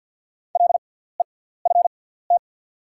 Morse code, heard as a keyed tone, about 48 wpm, spaced as for 17 wpm, 690 Hz. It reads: LEFT
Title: 6x Character Spacing - Top 500 Words - 50wpm